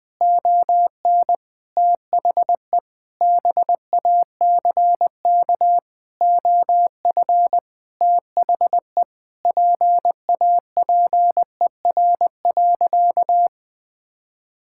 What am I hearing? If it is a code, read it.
Morse code, heard as a keyed tone, about 20 wpm, 700 Hz. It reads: ON THE BACK OF THE PAPER.
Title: Sentences with 5-Word Phrases - 20wpm